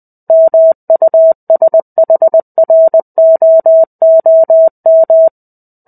Morse code, heard as a keyed tone, about 20 wpm, 650 Hz. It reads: MUSHROOM
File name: 05-Chapter-05 20 WPM